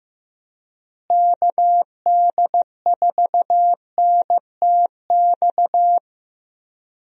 Morse code, heard as a keyed tone, about 15 wpm, 700 Hz. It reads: KD4NTX
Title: USA Call Signs - 6 Characters - Repeated 2x - 15wpm